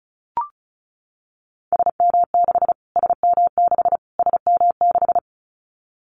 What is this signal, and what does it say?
Morse code, heard as a keyed tone, about 34 wpm, 700 Hz. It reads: SM6 SM6 SM6
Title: Callsign Prefixes for Top 25 DXCC - Repeated 3x - 35wpm